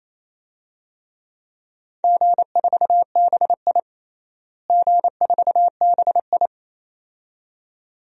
Morse code, heard as a keyed tone, about 28 wpm, 700 Hz. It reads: G4BI G4BI
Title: DXCC-100 Call Signs - 4 Characters - Repeated 2x - 28wpm